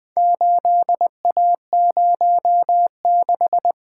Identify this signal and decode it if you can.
Morse code, heard as a keyed tone, about 24 wpm, 700 Hz. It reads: 8A06
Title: Speed-Racing - Four Letter-Number Combinations - 20wpm